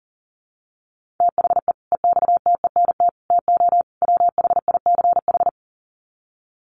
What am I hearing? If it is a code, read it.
Morse code, heard as a keyed tone, about 40 wpm, 700 Hz. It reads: THE EXTENT TO WHICH
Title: Sets of 4 Words 40wpm